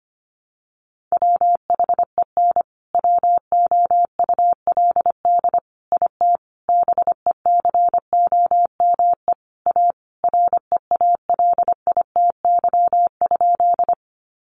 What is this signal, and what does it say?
Morse code, heard as a keyed tone, about 25 wpm, 700 Hz. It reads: WHEN WOULD IT BECOME A REALITY?
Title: Sentences from Top 1200 Words 25wpm